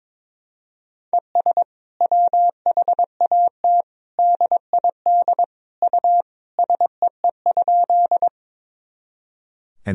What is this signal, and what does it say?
Morse code, heard as a keyed tone, about 22 wpm, 700 Hz. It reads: ES WHAT DID U SEE?
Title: Sentences from Top 100 Words and Abbreviations 22wpm